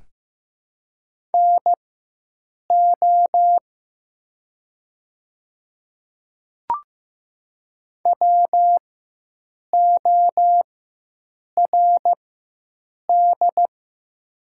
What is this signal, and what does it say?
Morse code, heard as a keyed tone, about 15 wpm, spaced as for 7 wpm, 700 Hz. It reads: NO WORD
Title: 4x Character Spacing - Top 100 Words - 15wpm